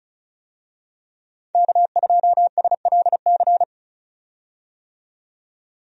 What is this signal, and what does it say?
Morse code, heard as a keyed tone, about 34 wpm, 700 Hz. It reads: K2SLC
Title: USA Call Signs - 5 Characters - 35wpm